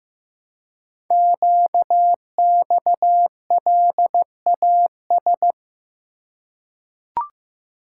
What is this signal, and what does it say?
Morse code, heard as a keyed tone, about 15 wpm, 700 Hz. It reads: QXLAS
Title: Code Groups - 5 Letters - 15wpm